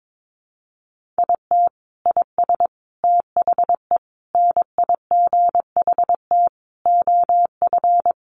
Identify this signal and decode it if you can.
Morse code, heard as a keyed tone, about 22 wpm, 700 Hz. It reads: IT IS THE NIGHT OF
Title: Sentences from Top 1400 Words 22wpm